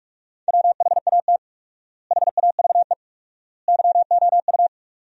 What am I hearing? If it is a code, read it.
Morse code, heard as a keyed tone, about 47 wpm, 700 Hz. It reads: WHAT HAVE YOU
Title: 4x Word-Spacing - Sentences from Top 200 Words - 45wpm